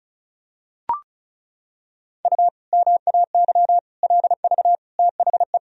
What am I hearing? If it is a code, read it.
Morse code, heard as a keyed tone, about 34 wpm, 700 Hz. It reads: U MAY LV THE
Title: Sentences from Top 400 Words and Abbreviations 35wpm